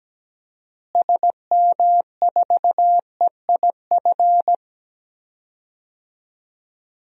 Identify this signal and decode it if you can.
Morse code, heard as a keyed tone, about 17 wpm, 700 Hz. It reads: SM4EIF
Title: DXCC-100 Call Signs - 6 Characters - Repeated 3x - 17wpm